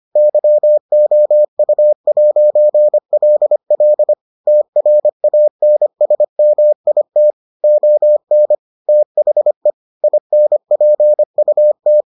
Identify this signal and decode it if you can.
Morse code, heard as a keyed tone, about 25 wpm, 600 Hz. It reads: YOU'LL TRANSMIT ON THE INPUT